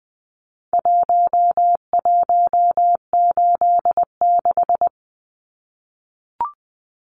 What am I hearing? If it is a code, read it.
Morse code, heard as a keyed tone, about 20 wpm, 700 Hz. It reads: 1186
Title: Numbers - Contest Style - 4-Digit 20wpm